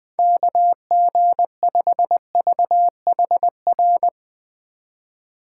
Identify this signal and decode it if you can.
Morse code, heard as a keyed tone, about 20 wpm, 700 Hz. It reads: KG5VHR
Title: USA Call Signs - 4x Character Spacing - 20wpm